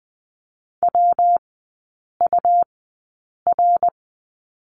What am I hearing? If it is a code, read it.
Morse code, heard as a keyed tone, about 20 wpm, spaced as for 8 wpm, 700 Hz. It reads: WUR
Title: 184 Lesson 30e - Letter b - Three Characters 20wpm